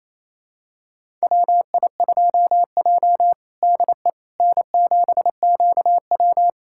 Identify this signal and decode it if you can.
Morse code, heard as a keyed tone, about 28 wpm, 700 Hz. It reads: WI2J DE N7QW
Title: USA Callsign DE Callsign - 4 Characters - 28wpm